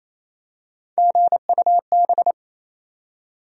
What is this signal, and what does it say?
Morse code, heard as a keyed tone, about 28 wpm, 700 Hz. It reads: GUB
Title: USA Call Sign Postfixes - 3 Letters 28wpm